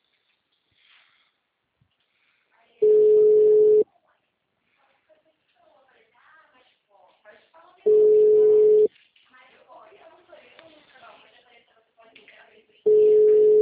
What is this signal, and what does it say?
Ringback tone (Europe, 425 Hz), with speech.
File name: out- -2034- - - .74791